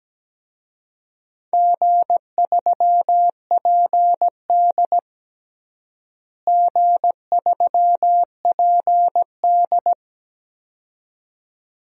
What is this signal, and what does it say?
Morse code, heard as a keyed tone, about 17 wpm, 700 Hz. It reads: G3PD G3PD
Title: DXCC-100 Call Signs - 4 Characters - Repeated 2x - 17wpm